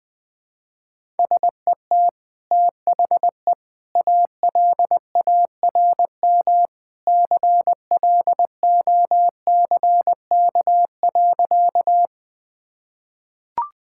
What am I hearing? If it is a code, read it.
Morse code, heard as a keyed tone, about 20 wpm, 700 Hz. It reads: SET THE ALARM CLOCK.
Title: Easy Sentences with Hints - 20wpm